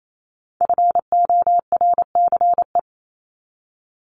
Morse code, heard as a keyed tone, about 28 wpm, 700 Hz. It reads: FORCE
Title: ED Suffix - Top 100 Words with Base Words 28wpm